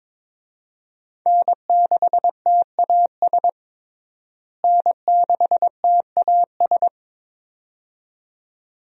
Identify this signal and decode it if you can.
Morse code, heard as a keyed tone, about 22 wpm, 700 Hz. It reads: N6TAS N6TAS
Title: USA Call Signs - 5 Characters - Repeated 2x - 22wpm